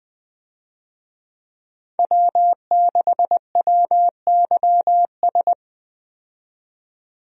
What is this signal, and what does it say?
Morse code, heard as a keyed tone, about 20 wpm, 700 Hz. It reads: W6WYS
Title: USA Call Signs - 5 Characters - 20wpm